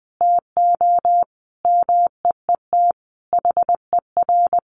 Morse code, heard as a keyed tone, about 20 wpm, 700 Hz. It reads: TO MEET HER
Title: Sentences from Top 1100 Words 20wpm